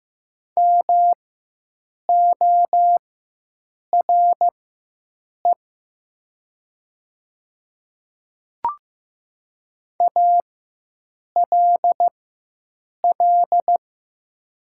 Morse code, heard as a keyed tone, about 15 wpm, spaced as for 7 wpm, 700 Hz. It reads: MORE ALL
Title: 4x Character Spacing - Top 100 Words - 15wpm